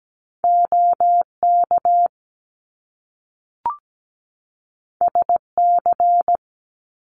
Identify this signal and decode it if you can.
Morse code, heard as a keyed tone, about 17 wpm, 700 Hz. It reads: OK SC
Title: US States 17wpm